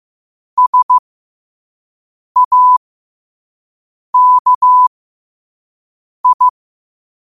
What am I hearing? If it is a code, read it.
Morse code, heard as a keyed tone, about 15 wpm, spaced as for 5 wpm, 1000 Hz. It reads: SAKI
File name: qotdfiles / 5wpm